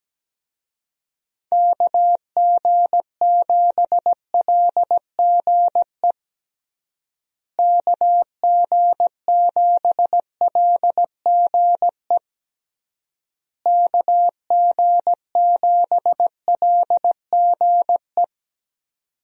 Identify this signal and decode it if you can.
Morse code, heard as a keyed tone, about 17 wpm, 700 Hz. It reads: KG7LGE KG7LGE KG7LGE
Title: USA Call Signs - 6 Characters - Repeated 3x - 17wpm